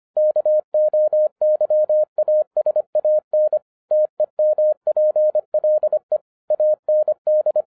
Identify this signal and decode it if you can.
Morse code, heard as a keyed tone, about 24 wpm, 610 Hz. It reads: KOYASAN TEMPLE AND
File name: 2011-10-mp3 / n04-JUST MOVED 25wpm